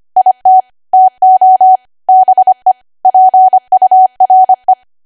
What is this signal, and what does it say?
Morse code, heard as a keyed tone, about 25 wpm, 750 Hz. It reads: IT TO BE PURE